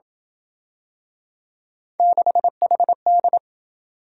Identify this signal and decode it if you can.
Morse code, heard as a keyed tone, about 26 wpm, 700 Hz. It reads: 6HD